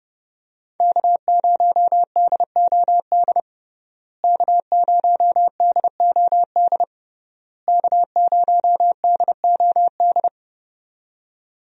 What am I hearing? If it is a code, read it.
Morse code, heard as a keyed tone, about 30 wpm, 700 Hz. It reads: K0DOD K0DOD K0DOD
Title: USA Call Signs - 5 Characters - Repeated 3x - 30wpm